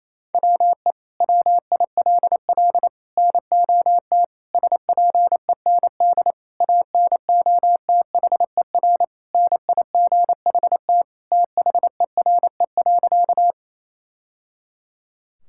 Morse code, heard as a keyed tone, about 28 wpm, 700 Hz. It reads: WE WILL NOT SPEND ANOTHER NIGHT THERE.
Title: Sentences from Top 1000 Words - Review Part 5 - 28wpm